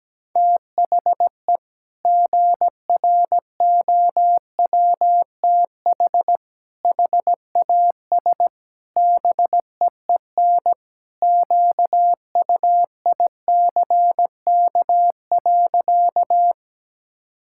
Morse code, heard as a keyed tone, about 17 wpm, 700 Hz. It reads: THE GROWTH HAS BEEN QUICK.
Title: Sentences from Top 1000 Words - Review Part 5 - 17wpm